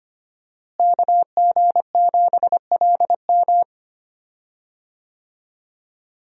Morse code, heard as a keyed tone, about 25 wpm, 700 Hz. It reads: KG7LM
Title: USA Call Signs - 5 Characters - Repeated 2x - 25wpm